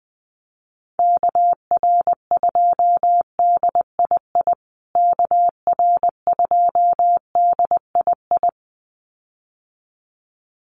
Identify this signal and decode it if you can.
Morse code, heard as a keyed tone, about 20 wpm, 700 Hz. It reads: KR2DII KR2DII
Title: USA Call Signs Repeated 2x 20wpm